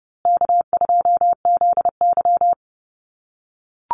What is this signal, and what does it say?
Morse code, heard as a keyed tone, about 30 wpm, 700 Hz. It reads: K2ZY
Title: USA Call Signs Repeated 2x 30wpm